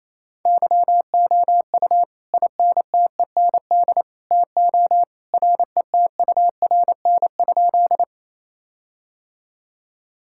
Morse code, heard as a keyed tone, about 28 wpm, 700 Hz. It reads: YOU INTEND TO RETURN?
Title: Sentences from Top 1600 Words 28wpm